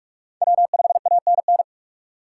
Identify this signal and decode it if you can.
Morse code, heard as a keyed tone, about 42 wpm, 700 Hz. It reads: W5ANN